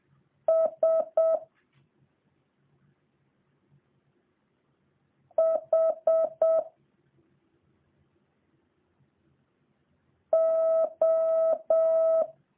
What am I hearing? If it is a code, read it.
Morse code, heard as a keyed tone, about 7 wpm, 650 Hz. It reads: S H O